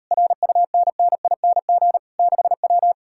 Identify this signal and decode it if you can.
Morse code, heard as a keyed tone, about 38 wpm, 700 Hz. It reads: RUNNING 6W